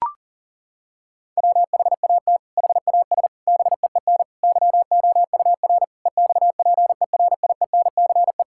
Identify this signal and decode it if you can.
Morse code, heard as a keyed tone, about 40 wpm, 700 Hz. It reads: WHAT HAS BEEN YOUR EXPERIENCE
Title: Sentences from Top 1000 Words - Review Part 6 - 40wpm